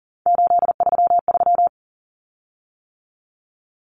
Morse code, heard as a keyed tone, about 40 wpm, 700 Hz. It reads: 833
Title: Common Cut Numbers - 3-Digit 40wpm